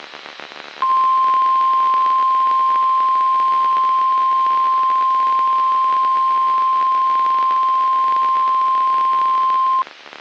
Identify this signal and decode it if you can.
Emergency Alert System (SAME) tones (the 1050 Hz attention tone). It is heard from 0.8 to 9.9 s.
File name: eas messages / EAS WXM95 Towanda